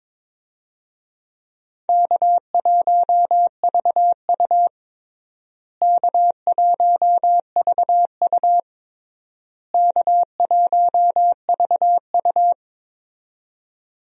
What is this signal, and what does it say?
Morse code, heard as a keyed tone, about 22 wpm, 700 Hz. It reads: K1VU K1VU K1VU
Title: USA Call Signs - 4 Characters - Repeated 3x - 22wpm